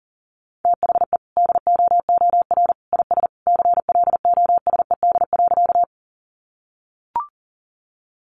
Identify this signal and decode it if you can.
Morse code, heard as a keyed tone, about 40 wpm, 700 Hz. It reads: THE DOOR IS CLOSED.
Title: Sentences from Top 1400 Words 40wpm